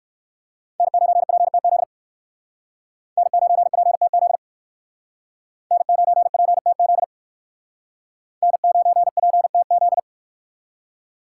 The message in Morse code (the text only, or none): N9PTZ N9PTZ N9PTZ N9PTZ